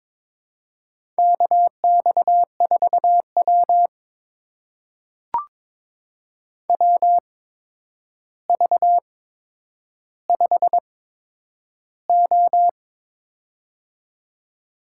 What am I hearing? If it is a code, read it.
Morse code, heard as a keyed tone, about 22 wpm, spaced as for 9 wpm, 700 Hz. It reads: KX4W WV5O
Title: USA Call Signs - 4 Characters - 8x Character Spacing - 22wpm